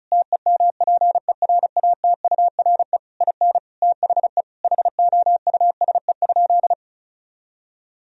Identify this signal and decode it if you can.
Morse code, heard as a keyed tone, about 36 wpm, 700 Hz. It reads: TEMPERATURE IN THE HOUSE?
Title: Sentences from Top 1000 Words - Review Part 4 - 35wpm